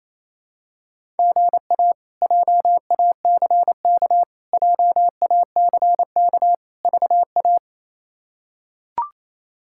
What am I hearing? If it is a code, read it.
Morse code, heard as a keyed tone, about 28 wpm, 700 Hz. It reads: GA JACK JACK VA
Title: K1USN SST QSO Practice - 28wpm